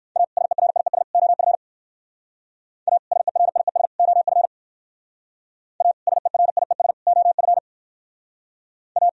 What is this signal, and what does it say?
Morse code, heard as a keyed tone, about 62 wpm, 700 Hz. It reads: A SERIES OF A SERIES OF A SERIES OF A